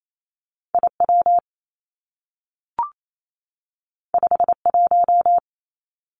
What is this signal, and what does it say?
Morse code, heard as a keyed tone, about 28 wpm, 700 Hz. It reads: I W 5 1
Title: Two Challenging Letter-Number Combinations 28wpm